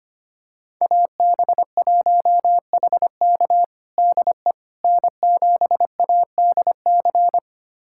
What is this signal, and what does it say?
Morse code, heard as a keyed tone, about 25 wpm, 700 Hz. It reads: AB1HK DE N7ADC
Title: USA Callsign DE Callsign - 5 Characters - 25wpm